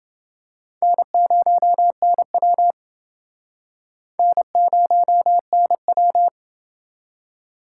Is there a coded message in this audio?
Yes: Morse code from 0.8 to 6.3 s, heard as a keyed tone, about 29 wpm, 700 Hz. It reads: N0NW N0NW